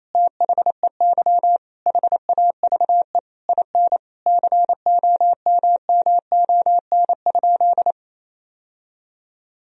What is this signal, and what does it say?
Morse code, heard as a keyed tone, about 28 wpm, 700 Hz. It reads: THEY HAVE IN COMMON?
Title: Sentences from Top 1000 Words - Review Part 3 - 28wpm